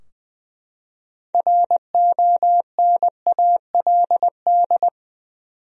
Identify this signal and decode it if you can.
Morse code, heard as a keyed tone, about 20 wpm, 700 Hz. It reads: RONALD
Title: USA Common Names - 20wpm